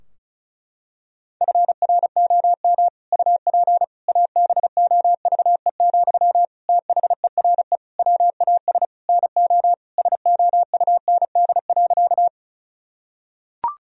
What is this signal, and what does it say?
Morse code, heard as a keyed tone, about 34 wpm, 700 Hz. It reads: FROM UP ABOVE, THERE WAS NO SOUND.
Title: Sentences from Top 1000 Words - Review Part 2 - 35wpm